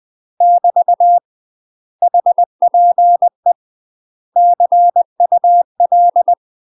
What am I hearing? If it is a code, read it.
Morse code, heard as a keyed tone, about 20 wpm, 700 Hz. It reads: <BT> HPE CUL
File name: UK MASTER QSOs TRAINING / 20WPM / UK MASTER QSOs 0044